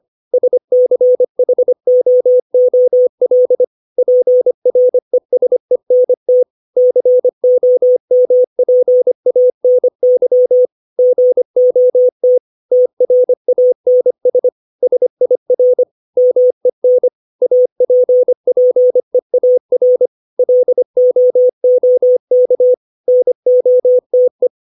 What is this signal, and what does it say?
Morse code, heard as a keyed tone, about 25 wpm, 500 Hz. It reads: SCHOOL PRESENT COMPANY GOT TRANS SIR MEN APPEAR LOOK NOTE
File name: 632 words / 632 Words 25WPM